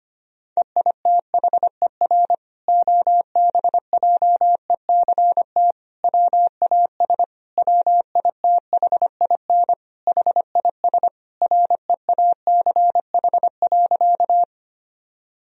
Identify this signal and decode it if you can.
Morse code, heard as a keyed tone, about 25 wpm, 700 Hz. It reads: EITHER OBJECT WAS WITHIN HIS REACH.
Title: Sentences from Top 1000 Words - Review Part 4 - 25wpm